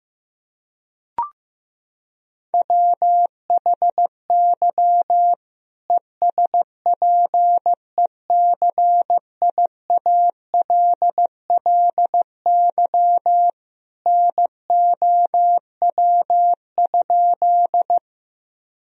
Morse code, heard as a keyed tone, about 15 wpm, 700 Hz. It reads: WHY ESPECIALLY NOW?
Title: Sentences from Top 1000 Words - Review Part 4 - 15wpm